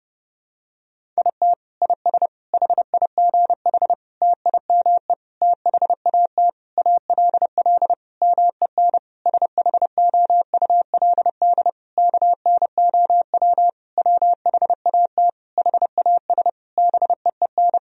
Morse code, heard as a keyed tone, about 30 wpm, 700 Hz. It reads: IT IS HIGH TIME THAT ALL MEN SHOULD KNOW WHAT HAS BEEN
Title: Sentences from Top 1000 Words - Review Part 1 - 30wpm